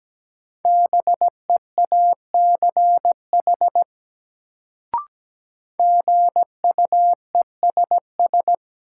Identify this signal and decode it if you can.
Morse code, heard as a keyed tone, about 17 wpm, 700 Hz. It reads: BEACH GUESS
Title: Sentences from Top 1500 Words - New Word Drill - 17wpm